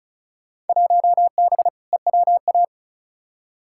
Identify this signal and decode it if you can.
Morse code, heard as a keyed tone, about 36 wpm, 700 Hz. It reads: 1B EWA
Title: Field Day Exchanges Encore - 35wpm